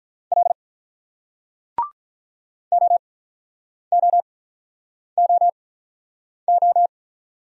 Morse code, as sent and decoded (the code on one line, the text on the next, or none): .-.. / --- --- --- ---
L OOOO